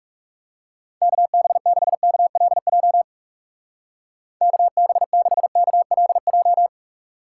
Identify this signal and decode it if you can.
Morse code, heard as a keyed tone, about 37 wpm, 700 Hz. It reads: KB6KLJ KB6KLJ
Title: Speed Racing - USA 2x3 Call Signs 30wpm